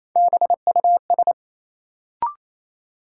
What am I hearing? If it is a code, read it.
Morse code, heard as a keyed tone, about 28 wpm, 700 Hz. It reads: BUS
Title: Sentences from Top 1000 Words - Review Part 6 - New Word Drill - 28wpm